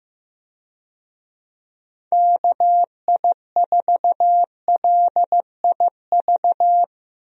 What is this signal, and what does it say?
Morse code, heard as a keyed tone, about 16 wpm, 700 Hz. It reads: KI4LIV
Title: USA Call Signs - 6 Characters - Repeated 3x - 15wpm